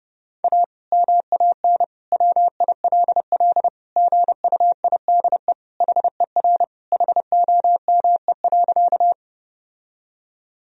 Morse code, heard as a keyed tone, about 32 wpm, 700 Hz. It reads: A MAN WILL GUIDE HER HOME.
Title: Sentences from Top 1000 Words - Review Part 3 - 30wpm